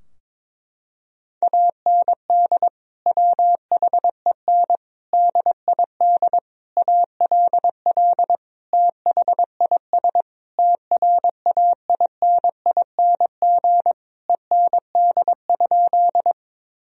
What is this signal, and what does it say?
Morse code, heard as a keyed tone, about 22 wpm, 700 Hz. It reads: AND WHEN DID ALL THIS TRAINING END?
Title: Sentences from Top 1000 Words - Review Part 4 - 22wpm